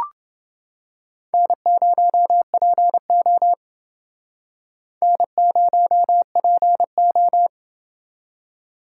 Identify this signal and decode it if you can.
Morse code, heard as a keyed tone, about 29 wpm, 690 Hz. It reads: N0PO N0PO